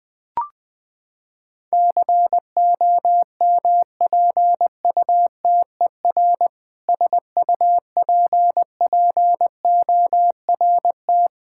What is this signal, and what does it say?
Morse code, heard as a keyed tone, about 20 wpm, 700 Hz. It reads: COMPUTER SUPPORT
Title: K1USN SST QSO Practice 20wpm